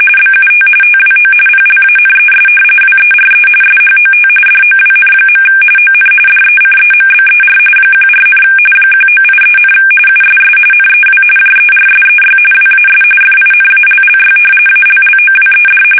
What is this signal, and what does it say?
RTTY, about 45.45 baud. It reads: BUT TWO PLANES WERE SHOT DOWN. ON DECEMBER 14, 1983, USS NEW JERSEY WAS FINALLY ALLOWED
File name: fsk850 / battleshipnj